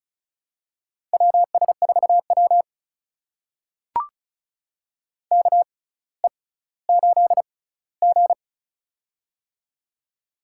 Morse code, heard as a keyed tone, about 36 wpm, spaced as for 18 wpm, 700 Hz. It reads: WS4W KE8G
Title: USA Call Signs - 4 Characters - 6x Character Spacing - 35wpm